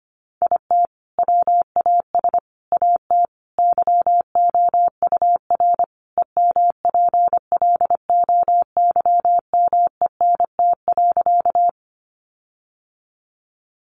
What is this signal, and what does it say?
Morse code, heard as a keyed tone, about 25 wpm, 700 Hz. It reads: IT WAS AT YOUR EMPLOYMENT.
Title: Sentences from Top 1500 Words 25wpm